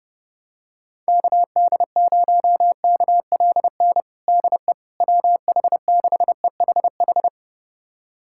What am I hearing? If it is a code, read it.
Morse code, heard as a keyed tone, about 30 wpm, 700 Hz. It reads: KD0KLN DE WH6EHH
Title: USA Callsign DE Callsign - 6 Characters - 30wpm